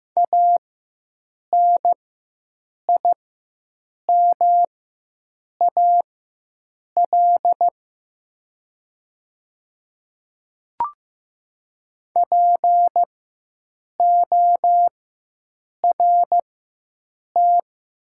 Morse code, heard as a keyed tone, about 15 wpm, spaced as for 7 wpm, 700 Hz. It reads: ANIMAL PORT
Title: 4x Character Spacing - Top 200 Words - 15wpm